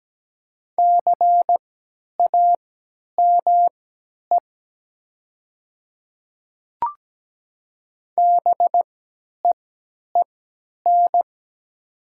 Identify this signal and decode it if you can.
Morse code, heard as a keyed tone, about 17 wpm, spaced as for 10 wpm, 700 Hz. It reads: CAME BEEN
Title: 3x Character Spacing - Top 200 Words - 17wpm